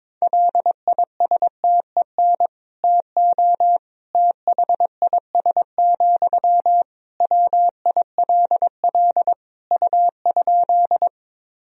Morse code, heard as a keyed tone, about 21 wpm, 700 Hz. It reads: LISTEN TO THIS, WILL U?
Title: Sentences from Top 400 Words and Abbreviations 22wpm